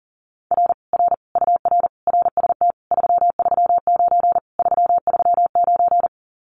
Morse code, heard as a keyed tone, about 40 wpm, 700 Hz. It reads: R R UR RST 339 339
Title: Common QSO Phrases Encore 40wpm